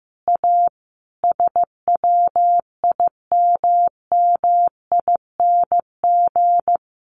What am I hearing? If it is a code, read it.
Morse code, heard as a keyed tone, about 15 wpm, 700 Hz. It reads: A SWIMMING
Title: Additional Easy Phrases 15wpm